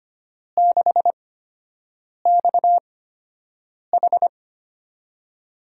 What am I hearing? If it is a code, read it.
Morse code, heard as a keyed tone, about 25 wpm, spaced as for 7 wpm, 700 Hz. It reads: 6XH
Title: USA Call Signs - 5 Characters - 8x Character Spacing - 25wpm